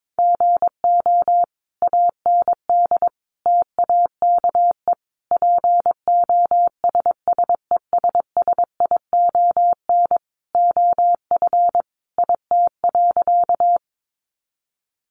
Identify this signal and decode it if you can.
Morse code, heard as a keyed tone, about 22 wpm, 700 Hz. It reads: GO AND TAKE POSSESSION OF IT.
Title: Sentences from Top 1400 Words 22wpm